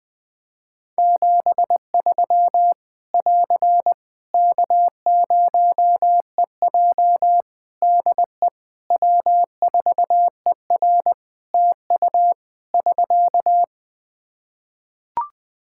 Morse code, heard as a keyed tone, about 20 wpm, 700 Hz. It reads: 73 <AR> K0EJ DE W4ER TU <SK>
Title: Practice QSO - Exchange 4 - Ending - 20wpm